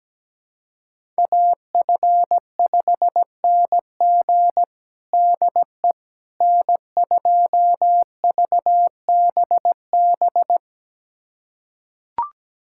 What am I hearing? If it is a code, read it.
Morse code, heard as a keyed tone, about 17 wpm, 700 Hz. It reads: AF5NG DE N2VBB
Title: USA Callsign DE Callsign - 5 Characters - 17wpm